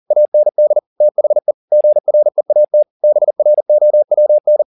Morse code, heard as a keyed tone, about 40 wpm, 600 Hz. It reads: AND THE GREAT BROWN